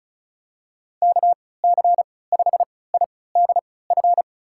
Morse code, heard as a keyed tone, about 34 wpm, spaced as for 20 wpm, 700 Hz. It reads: KC5IDF
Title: USA Call Signs - 6 Characters - 3x Character Spacing - 35wpm